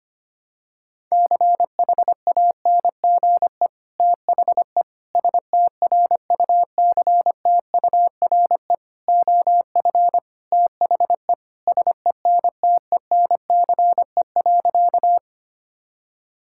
Morse code, heard as a keyed tone, about 25 wpm, 700 Hz. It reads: CHANGE THE STRUCTURE OF THE SENTENCE.
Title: Sentences from Top 1000 Words - Review Part 6 - 25wpm